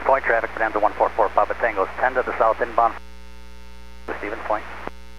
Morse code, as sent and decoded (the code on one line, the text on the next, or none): none